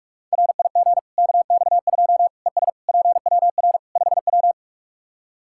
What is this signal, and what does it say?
Morse code, heard as a keyed tone, about 45 wpm, 700 Hz. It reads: RIG KX2 ES PWR 5W
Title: Practice QSO - Exchange 2 - Info - 45wpm